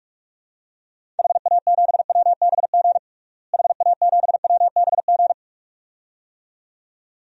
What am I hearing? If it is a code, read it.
Morse code, heard as a keyed tone, about 46 wpm, 700 Hz. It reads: HA7WBG HA7WBG
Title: DXCC-100 Call Signs - 6 Characters - Repeated 2x - 45wpm